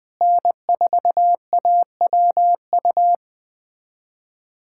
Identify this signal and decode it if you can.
Morse code, heard as a keyed tone, about 20 wpm, 700 Hz. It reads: N4AWU
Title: USA Callsign DE Callsign - 5 Characters - 20wpm